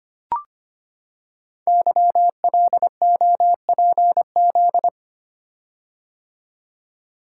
Morse code, heard as a keyed tone, about 25 wpm, 700 Hz. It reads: YLOPZ
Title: Code Groups - 5 Letters - 25wpm